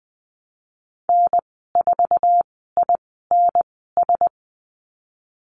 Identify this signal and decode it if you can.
Morse code, heard as a keyed tone, about 20 wpm, spaced as for 15 wpm, 700 Hz. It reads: N4INS
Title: 062 Lesson 11g - Letter h - Call Signs 20wpm